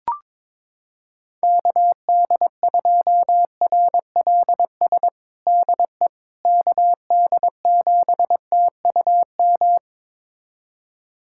Morse code, heard as a keyed tone, about 21 wpm, 700 Hz. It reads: KD2RLS DE KD7TUM
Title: USA Callsign DE Callsign - 6 Characters - 22wpm